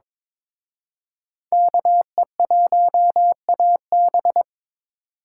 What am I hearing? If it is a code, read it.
Morse code, heard as a keyed tone, about 26 wpm, 700 Hz. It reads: KE1AB
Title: Speed Racing - USA 2x2 Call Signs 22wpm